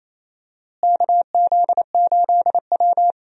Morse code, heard as a keyed tone, about 28 wpm, 700 Hz. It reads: KZ8W
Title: USA Call Signs - 4 Characters - Repeated 3x - 28wpm